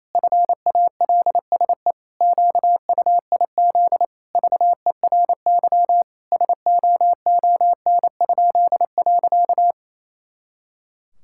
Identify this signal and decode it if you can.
Morse code, heard as a keyed tone, about 28 wpm, 700 Hz. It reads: FALSE QUIZ VERY SOON?.
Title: Pangram Challenge v2 - 28wpm